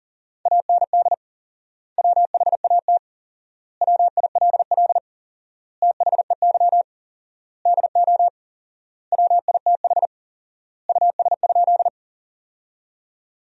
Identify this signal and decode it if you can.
Morse code, heard as a keyed tone, about 37 wpm, 700 Hz. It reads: AND WHAT WILL THEY DO WITH US?
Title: 4x Word-Spacing - Sentences from Top 200 Words - 40wpm